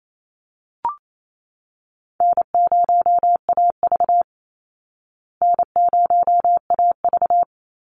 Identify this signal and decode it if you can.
Morse code, heard as a keyed tone, about 28 wpm, 700 Hz. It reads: N0AV N0AV
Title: Easy Call Signs Repeated 2x - 28wpm